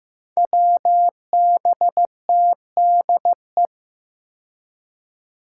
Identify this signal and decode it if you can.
Morse code, heard as a keyed tone, about 15 wpm, 700 Hz. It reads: WBTDE
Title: Five Challenging Letter-Number Combinations 15wpm